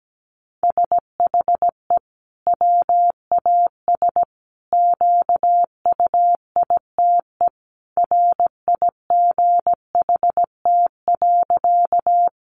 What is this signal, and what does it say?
Morse code, heard as a keyed tone, about 17 wpm, 700 Hz. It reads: SHE WAS QUITE RIGHT.
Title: Sentences from Top 700 Words 17wpm